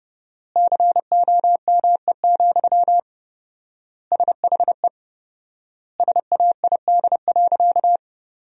Morse code, heard as a keyed tone, about 30 wpm, 700 Hz. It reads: COME, SHE SAID.
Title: 4x Word-Spacing - Sentences from Top 100 Words - 30wpm